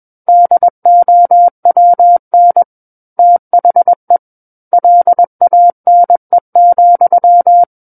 Morse code, heard as a keyed tone, about 21 wpm, 700 Hz. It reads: DOWN THE LANE,
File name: Book One / 12-Chapter 12 - 21 WPM